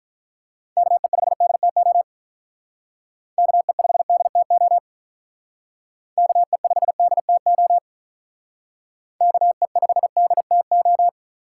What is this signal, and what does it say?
Morse code, heard as a keyed tone, about 46 wpm, 700 Hz. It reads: KE5DTO KE5DTO KE5DTO KE5DTO